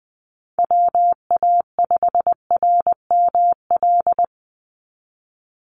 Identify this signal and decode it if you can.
Morse code, heard as a keyed tone, about 20 wpm, 700 Hz. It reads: WA5RML
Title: Call Signs - 20wpm